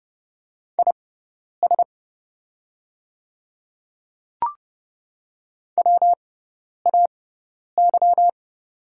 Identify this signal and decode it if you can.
Morse code, heard as a keyed tone, about 30 wpm, spaced as for 10 wpm, 700 Hz. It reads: IS WAY
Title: 6x Character Spacing - Top 100 Words - 30wpm